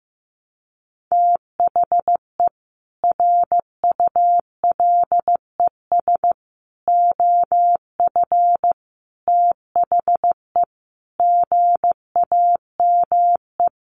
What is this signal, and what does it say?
Morse code, heard as a keyed tone, about 15 wpm, 700 Hz. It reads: THE RULES OF THE GAME
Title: Sets of 5 Words Encore2 15wpm